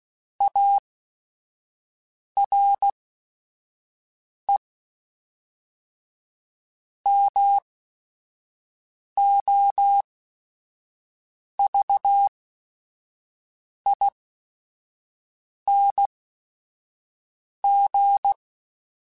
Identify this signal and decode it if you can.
Morse code, heard as a keyed tone, about 16 wpm, spaced as for 5 wpm, 800 Hz. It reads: ARE MOVING
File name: QSOs / QSOs-mp3s / WD0BGP 1@5